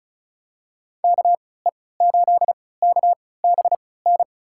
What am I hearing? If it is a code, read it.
Morse code, heard as a keyed tone, about 36 wpm, spaced as for 20 wpm, 700 Hz. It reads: KE8KBN
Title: USA Call Signs - 6 Characters - 3x Character Spacing - 35wpm